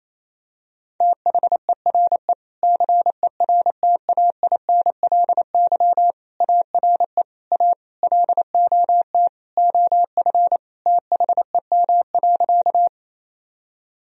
Morse code, heard as a keyed tone, about 28 wpm, 700 Hz. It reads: THERE CERTAINLY ARE A LOT OF THEM.
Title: Sentences from Top 1000 Words - Review Part 3 - 28wpm